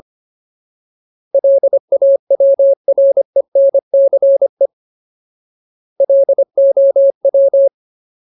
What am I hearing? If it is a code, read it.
Morse code, heard as a keyed tone, about 25 wpm, 550 Hz. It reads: LAWRENCE LOW
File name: Ultimate Word List / 25WPM Ultimate